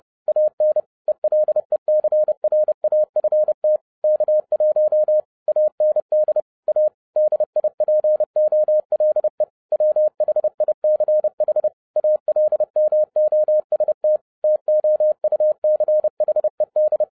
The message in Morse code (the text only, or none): AN ELECRAFT K1 AND A DIPOLE WHICH ALMOST TOUCHED